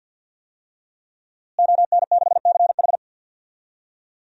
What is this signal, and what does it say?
Morse code, heard as a keyed tone, about 50 wpm, 700 Hz. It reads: ON6CH
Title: DXCC-100 Call Signs - 5 Characters - Repeated 3x - 50wpm